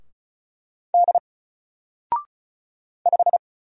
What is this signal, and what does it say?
Morse code, heard as a keyed tone, about 34 wpm, 700 Hz. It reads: D 5
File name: Single-Letter-Number-35wpm